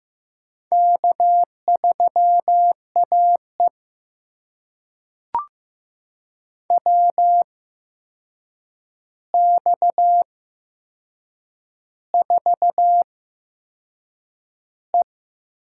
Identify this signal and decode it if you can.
Morse code, heard as a keyed tone, about 15 wpm, spaced as for 6 wpm, 700 Hz. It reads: K3AE WX4E
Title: USA Call Signs - 4 Characters - 8x Character Spacing - 15wpm